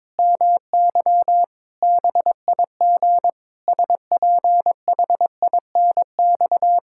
Morse code, heard as a keyed tone, about 22 wpm, 700 Hz. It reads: MY BIG SPHINX
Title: Pangram Challenge v2 - 22wpm